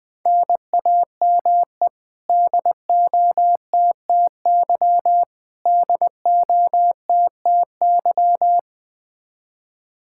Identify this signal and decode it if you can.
Morse code, heard as a keyed tone, about 20 wpm, 700 Hz. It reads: NAME DOTTY DOTTY
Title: Practice QSO - Exchange 1 - Beginning - 20wpm